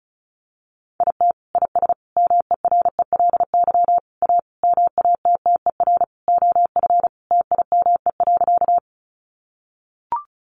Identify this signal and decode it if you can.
Morse code, heard as a keyed tone, about 34 wpm, 700 Hz. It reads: IT IS MERELY A MATTER OF TIME.
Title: Sentences from Top 1400 Words 35wpm